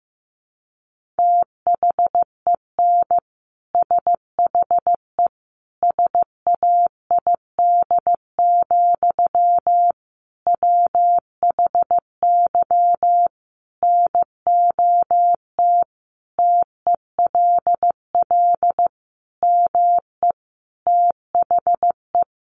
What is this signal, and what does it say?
Morse code, heard as a keyed tone, about 15 wpm, 700 Hz. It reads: THEN SHE SAID, WHY NOT TELL ME THE
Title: Sentences from Top 800 Words 15wpm